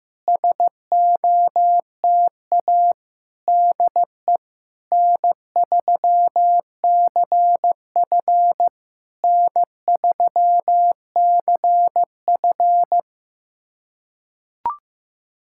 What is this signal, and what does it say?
Morse code, heard as a keyed tone, about 15 wpm, 700 Hz. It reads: SOTA DE N3CF N3CF
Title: SOTA QSO Practice 15wpm